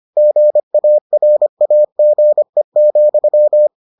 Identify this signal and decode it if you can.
Morse code, heard as a keyed tone, about 25 wpm, 600 Hz. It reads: GARAGE,